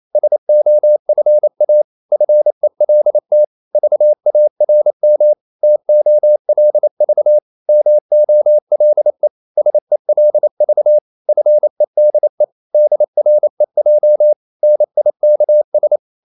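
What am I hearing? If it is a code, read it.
Morse code, heard as a keyed tone, about 28 wpm, 600 Hz. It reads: SOFA FELT VARM TOLV MOLE SELV FEDE DREJ NIKS
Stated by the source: Danish short words sent in Morse at 28 wpm, from CW med Gnister.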